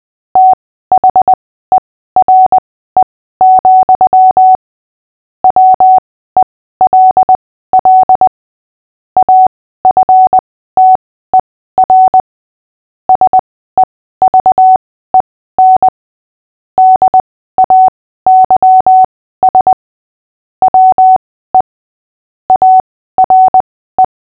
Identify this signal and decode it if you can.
Morse code, heard as a keyed tone, about 20 wpm, spaced as for 14 wpm, 750 Hz. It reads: THERE, WELL AFTER SEVEN DAYS WE ARE
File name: cw / fists / Short-Story-8-14wpm